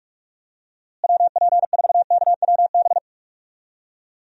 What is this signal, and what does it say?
Morse code, heard as a keyed tone, about 41 wpm, 700 Hz. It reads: WP4KWB
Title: USA Call Signs - 6 Characters - Repeated 3x - 45wpm